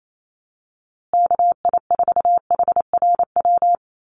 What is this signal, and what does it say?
Morse code, heard as a keyed tone, about 28 wpm, 700 Hz. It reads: KI4HRW
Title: USA Call Signs Repeated 3x 28wpm